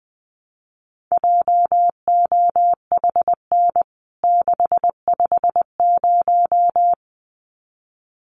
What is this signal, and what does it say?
Morse code, heard as a keyed tone, about 20 wpm, 700 Hz. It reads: JOHN 650
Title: CWT Exchange 20wpm